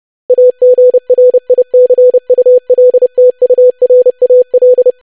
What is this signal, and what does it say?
Morse code, heard as a keyed tone, about 30 wpm, 500 Hz. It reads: AGRICULTURAL
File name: lesson / web / longwords-25wpm